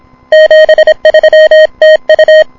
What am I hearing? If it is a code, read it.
Morse code, heard as a keyed tone, about 26 wpm, 1850 Hz, with noise. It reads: 73 TU
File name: n6tr / K1NAandN6TRatHA5JI on 160M 09 09 26 AM0308 1